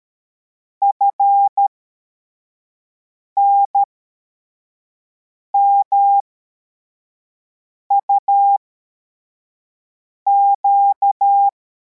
Morse code, heard as a keyed tone, about 13 wpm, spaced as for 4 wpm, 800 Hz. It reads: FNMUQ
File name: formation / cw / Text84